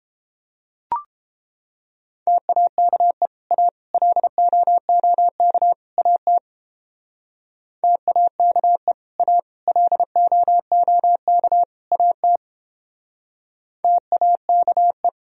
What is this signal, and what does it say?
Morse code, heard as a keyed tone, about 29 wpm, 700 Hz. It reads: TAKE A LOOK AT TAKE A LOOK AT TAKE